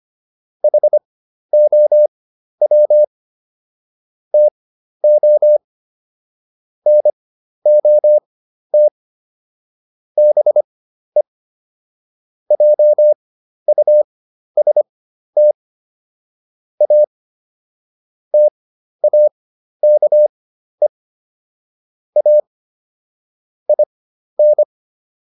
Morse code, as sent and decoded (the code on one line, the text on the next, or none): .... --- .-- / - --- / -. --- - / -... / .--- ..- ... - / .- / - .- -.- . / .- / .. -.
HOW TO NOT BE JUST A TAKE A IN